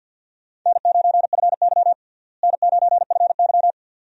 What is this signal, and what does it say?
Morse code, heard as a keyed tone, about 46 wpm, 700 Hz. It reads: N9FY N9FY
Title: USA Call Signs - 4 Characters - Repeated 2x - 50wpm